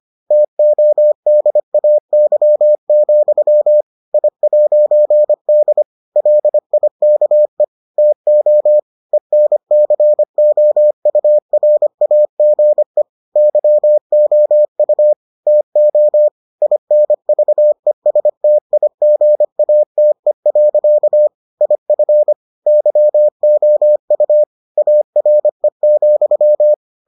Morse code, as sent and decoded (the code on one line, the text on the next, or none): - --- -.. .- -.-- --..-- / .. .----. -.. / .-.. .. -.- . / - --- / . -. -.-. --- ..- .-. .- --. / -.-- --- ..- / - --- / .. -. ...- . ... - .. --. .- - . .-.-.- / .. ..-. / -.-- --- ..- / .- .-. --..--
TODAY, I'D LIKE TO ENCOURAGE YOU TO INVESTIGATE. IF YOU ARE,